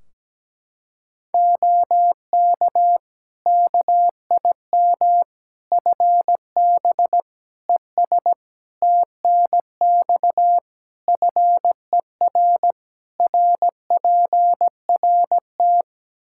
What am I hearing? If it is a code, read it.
Morse code, heard as a keyed tone, about 17 wpm, 700 Hz. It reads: OK KIM FB ES TNX FER RPRT
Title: Practice QSO - Exchange 2 - Info - 17wpm